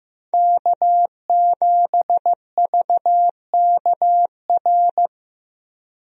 Morse code, heard as a keyed tone, about 15 wpm, 700 Hz. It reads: K7VKR
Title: USA Call Signs - 5 Characters - Repeated 2x - 15wpm